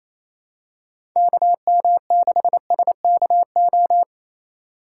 Morse code, heard as a keyed tone, about 28 wpm, 700 Hz. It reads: KM6SKO
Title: USA Call Signs - 6 Characters - 4x Character Spacing - 28wpm